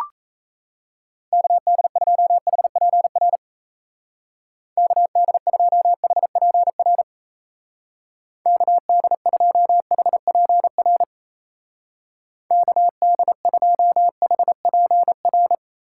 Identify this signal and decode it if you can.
Morse code, heard as a keyed tone, about 34 wpm, 700 Hz. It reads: KD2HPR KD2HPR KD2HPR KD2HPR
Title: Speed Racing - USA 2x3 Call Signs 28wpm